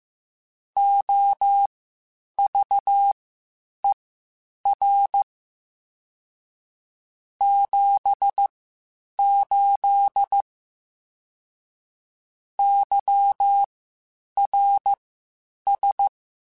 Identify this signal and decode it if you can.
Morse code, heard as a keyed tone, about 15 wpm, spaced as for 8 wpm, 800 Hz. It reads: OVER 78 YRS
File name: (spaced) / QSOs Spaced-mp3s / kc7haa @13 sp